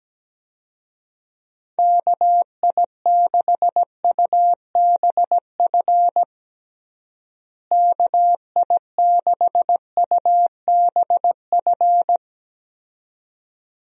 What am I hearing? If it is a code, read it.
Morse code, heard as a keyed tone, about 17 wpm, 700 Hz. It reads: KI6UBF KI6UBF
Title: USA Call Signs - 6 Characters - Repeated 2x - 17wpm